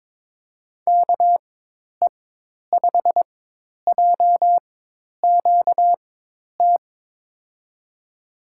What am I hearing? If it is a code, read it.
Morse code, heard as a keyed tone, about 22 wpm, spaced as for 10 wpm, 700 Hz. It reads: KE5JQT
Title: USA Call Signs - 4x Character Spacing - 22wpm